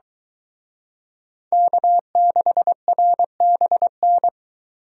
Morse code, heard as a keyed tone, about 24 wpm, 700 Hz. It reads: K6RBN